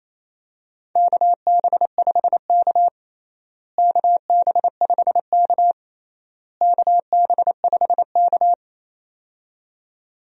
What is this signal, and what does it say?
Morse code, heard as a keyed tone, about 27 wpm, 700 Hz. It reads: KB5K KB5K KB5K
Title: USA Call Signs - 4 Characters - Repeated 3x - 28wpm